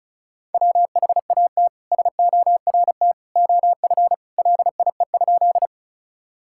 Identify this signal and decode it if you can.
Morse code, heard as a keyed tone, about 34 wpm, 700 Hz. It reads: WHAT SORT OF LIE?
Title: Sentences from Top 1600 Words 35wpm